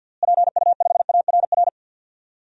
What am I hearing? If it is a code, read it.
Morse code, heard as a keyed tone, about 51 wpm, 700 Hz. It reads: PU5ARR
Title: DXCC-100 Call Signs - 6 Characters - Repeated 2x - 50wpm